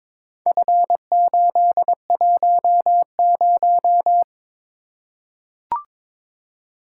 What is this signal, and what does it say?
Morse code, heard as a keyed tone, about 22 wpm, 700 Hz. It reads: F810
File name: practice / 22wpm / Four Letter-Number Combinations - 22wpm